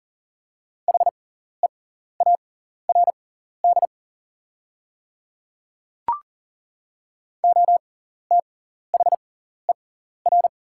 Morse code, heard as a keyed tone, about 39 wpm, spaced as for 14 wpm, 690 Hz. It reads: HEARD OTHER
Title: 6x Character Spacing - Top 500 Words - 40wpm